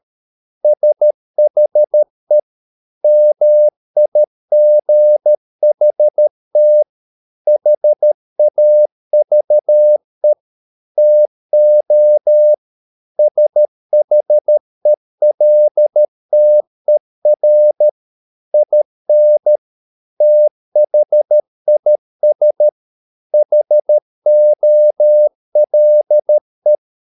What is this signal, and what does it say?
Morse code, heard as a keyed tone, about 13 wpm, 600 Hz. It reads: SHE MIGHT HAVE TO SHELTER IN THIS HOLE